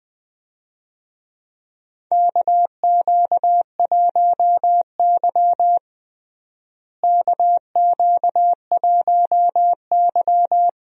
Morse code, heard as a keyed tone, about 21 wpm, 700 Hz. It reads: KQ1Y KQ1Y
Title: USA Call Signs - 4 Characters - Repeated 2x - 20wpm